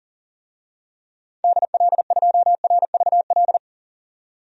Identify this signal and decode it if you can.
Morse code, heard as a keyed tone, about 40 wpm, 700 Hz. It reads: DL2RVL
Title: DXCC-100 Call Signs - 6 Characters - 40wpm